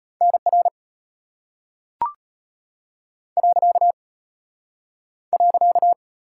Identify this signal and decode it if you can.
Morse code, heard as a keyed tone, about 34 wpm, 700 Hz. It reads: NR .